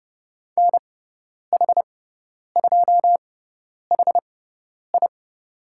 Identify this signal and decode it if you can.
Morse code, heard as a keyed tone, about 30 wpm, spaced as for 10 wpm, 700 Hz. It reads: NH2HI
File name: Lesson 16g - Letter c - Call Signs 30wpm